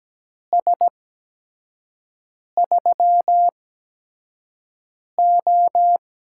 Morse code, heard as a keyed tone, about 17 wpm, spaced as for 5 wpm, 700 Hz. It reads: S3O
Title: USA Call Signs - 4 Characters - 8x Character Spacing - 17wpm